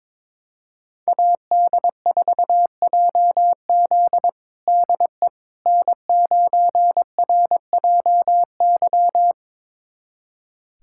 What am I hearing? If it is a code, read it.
Morse code, heard as a keyed tone, about 22 wpm, 700 Hz. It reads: AD4JZ DE N9RJY
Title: USA Callsign DE Callsign - 5 Characters - 22wpm